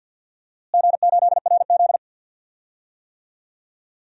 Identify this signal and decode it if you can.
Morse code, heard as a keyed tone, about 52 wpm, 700 Hz. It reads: G8RZ
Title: DXCC-100 Call Signs - 4 Characters - Repeated 3x - 50wpm